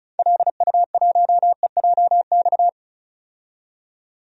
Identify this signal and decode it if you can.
Morse code, heard as a keyed tone, about 36 wpm, 700 Hz. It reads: LU1EJX
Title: DXCC-100 Call Signs - 6 Characters - 35wpm